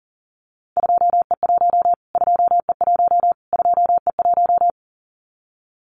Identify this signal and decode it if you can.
Morse code, heard as a keyed tone, about 40 wpm, 700 Hz. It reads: 2E1 2E1 2E1
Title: Callsign Prefixes for Top 25 DXCC - Repeated 3x - 40wpm